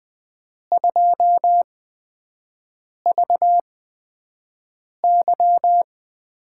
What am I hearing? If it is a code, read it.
Morse code, heard as a keyed tone, about 20 wpm, spaced as for 5 wpm, 700 Hz. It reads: 2VY
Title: USA Call Signs - 5 Characters - 8x Character Spacing - 20wpm